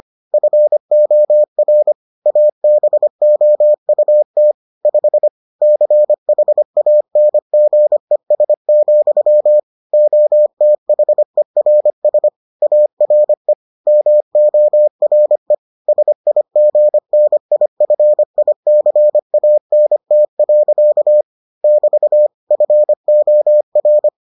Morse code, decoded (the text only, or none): FOR ABOUT 5 CHANGES, OTHERS ARE MORE SIGNIFICANT. <BT> FOR